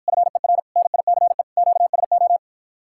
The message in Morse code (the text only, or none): FER NICE QSO